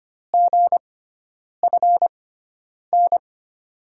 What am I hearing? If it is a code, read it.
Morse code, heard as a keyed tone, about 25 wpm, spaced as for 9 wpm, 700 Hz. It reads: GFN
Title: USA Call Signs - 6 Characters - 6x Character Spacing - 25wpm